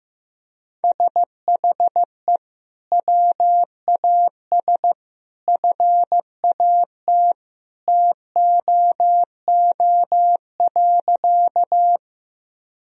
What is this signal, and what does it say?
Morse code, heard as a keyed tone, about 16 wpm, 700 Hz. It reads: SHE WAS FAT TOO.
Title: Sentences from Top 1000 Words - Review Part 4 - 15wpm